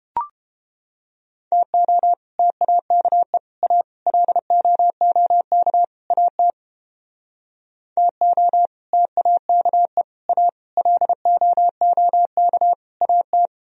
Morse code, heard as a keyed tone, about 29 wpm, 700 Hz. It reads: TO TAKE A LOOK AT TO TAKE A LOOK AT